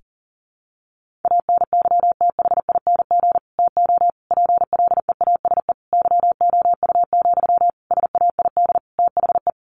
Morse code, heard as a keyed tone, about 40 wpm, 700 Hz. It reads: ANYTHING TO PLEASE YOU, SAID THE
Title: Sentences from Top 1500 Words 40wpm